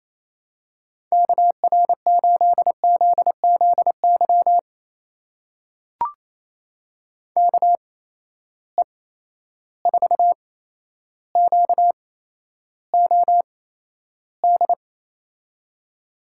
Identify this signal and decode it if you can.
Morse code, heard as a keyed tone, about 28 wpm, spaced as for 12 wpm, 700 Hz. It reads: KR8ZZY KE4QOD
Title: USA Call Signs - 6 Characters - 8x Character Spacing - 28wpm